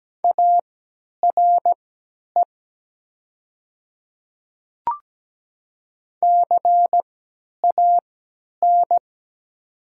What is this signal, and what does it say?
Morse code, heard as a keyed tone, about 17 wpm, spaced as for 10 wpm, 700 Hz. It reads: ARE CAN
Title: 3x Character Spacing - Top 300 Words - 17wpm